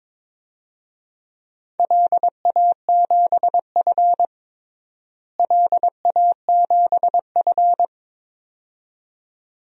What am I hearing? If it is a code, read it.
Morse code, heard as a keyed tone, about 22 wpm, 700 Hz. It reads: LA7F LA7F
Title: DXCC-100 Call Signs - 4 Characters - Repeated 2x - 22wpm